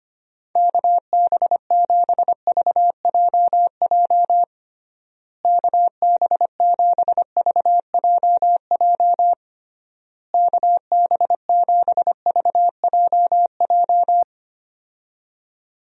Morse code, heard as a keyed tone, about 25 wpm, 700 Hz. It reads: KB7VJJ KB7VJJ KB7VJJ
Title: USA Call Signs - 6 Characters - Repeated 3x - 25wpm